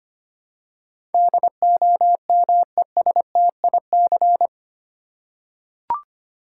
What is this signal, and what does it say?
Morse code, heard as a keyed tone, about 25 wpm, 700 Hz. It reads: DOMESTIC
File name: Eight Letter Words - 25wpm